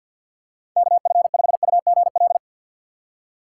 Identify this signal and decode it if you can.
Morse code, heard as a keyed tone, about 47 wpm, 700 Hz. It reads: KF5UGL
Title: USA Call Signs - 6 Characters - 3x Character Spacing - 50wpm